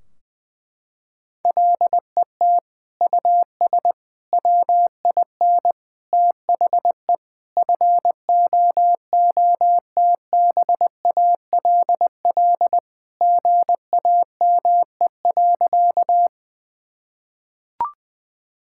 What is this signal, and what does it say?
Morse code, heard as a keyed tone, about 20 wpm, 700 Hz. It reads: LET US WIN THE FOOTBALL GAME.
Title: Sentences from Top 1600 Words 20wpm